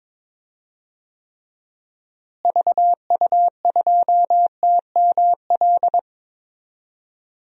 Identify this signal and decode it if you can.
Morse code, heard as a keyed tone, about 22 wpm, 700 Hz. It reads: VU2TML
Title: DXCC-100 Call Signs - 6 Characters - Repeated 2x - 22wpm